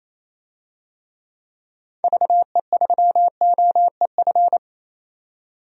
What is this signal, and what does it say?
Morse code, heard as a keyed tone, about 28 wpm, 700 Hz. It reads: VE3OEF
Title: DXCC-100 Call Signs - 6 Characters - 28wpm